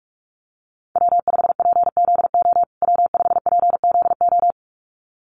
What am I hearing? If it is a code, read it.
Morse code, heard as a keyed tone, about 44 wpm, 700 Hz. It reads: W5PZO W5PZO
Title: USA Call Signs Repeated 2x 45wpm